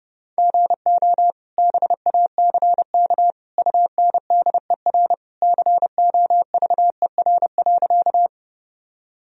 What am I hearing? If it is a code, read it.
Morse code, heard as a keyed tone, about 30 wpm, 700 Hz. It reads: GO BACK UNDER COVER.
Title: Sentences from Top 1000 Words - Review Part 1 - 30wpm